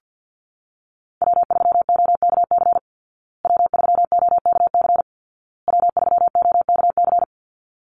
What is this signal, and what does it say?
Morse code, heard as a keyed tone, about 50 wpm, 700 Hz. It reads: W3OKC W3OKC W3OKC
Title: Call Signs Repeated 2x - 50wpm